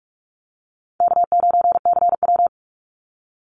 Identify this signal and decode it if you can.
Morse code, heard as a keyed tone, about 46 wpm, 700 Hz. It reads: K9CW
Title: Easy Call Signs Repeated 3x - 45wpm